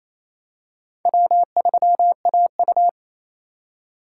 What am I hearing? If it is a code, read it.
Morse code, heard as a keyed tone, about 28 wpm, 700 Hz. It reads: W3AU
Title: USA Call Signs - 2x Character Spacing - 28wpm